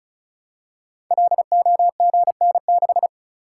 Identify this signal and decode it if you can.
Morse code, heard as a keyed tone, about 35 wpm, 690 Hz. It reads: LOGN6
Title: Code Groups - 5 Letter:Numbers - 35wpm